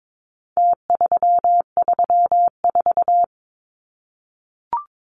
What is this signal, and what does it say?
Morse code, heard as a keyed tone, about 22 wpm, 700 Hz. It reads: T334
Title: Numbers - Contest Style - 4-Digit 22wpm